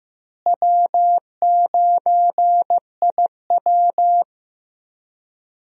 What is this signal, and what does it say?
Morse code, heard as a keyed tone, about 15 wpm, 700 Hz. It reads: W9IW
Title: USA Call Signs - 4 Characters - Repeated 2x - 15wpm